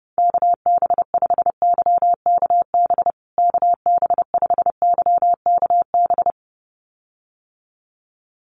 Morse code, heard as a keyed tone, about 30 wpm, 700 Hz. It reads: KB5YKB KB5YKB
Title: USA Call Signs Repeated 2x 30wpm